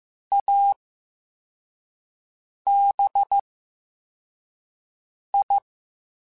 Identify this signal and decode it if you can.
Morse code, heard as a keyed tone, about 15 wpm, 800 Hz. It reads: A B I
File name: Special INGs-mp3s / Special INGs-mp3s / SlowToFaster ing @10